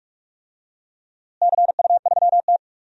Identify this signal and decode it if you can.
Morse code, heard as a keyed tone, about 46 wpm, 700 Hz. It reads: CU3T
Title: DXCC-100 Call Signs - 4 Characters - Repeated 2x - 45wpm